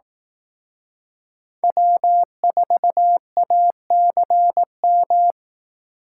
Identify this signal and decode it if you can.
Morse code, heard as a keyed tone, about 18 wpm, 700 Hz. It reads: W4ACM
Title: Speed Racing - USA 1x3 Call Signs 15wpm